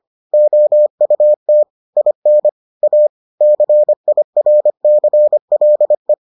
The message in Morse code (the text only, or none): OUT IN A CIRCLE